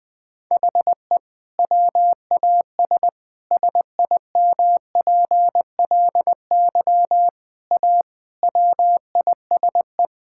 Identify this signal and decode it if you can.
Morse code, heard as a keyed tone, about 20 wpm, 700 Hz. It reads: HE WAS SIMPLY A WISE
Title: Sentences from Top 1600 Words 20wpm